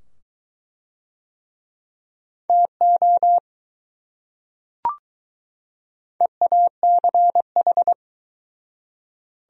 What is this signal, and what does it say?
Morse code, heard as a keyed tone, about 20 wpm, 700 Hz. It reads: TO EACH